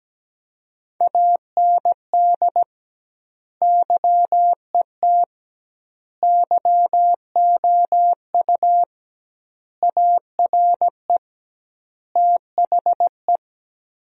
Morse code, heard as a keyed tone, about 17 wpm, 700 Hz. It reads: AND YET YOU ARE THE
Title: 2x Word-Spacing - Sentences from Top 500 Words - 17wpm